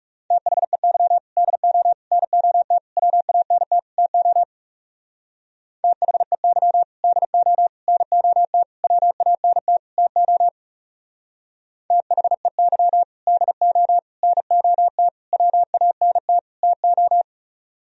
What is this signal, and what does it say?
Morse code, heard as a keyed tone, about 35 wpm, 700 Hz. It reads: THEY DO NOT WANT TO THEY DO NOT WANT TO THEY DO NOT WANT TO
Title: Speed-Racing - Sets of 5 Words 30wpm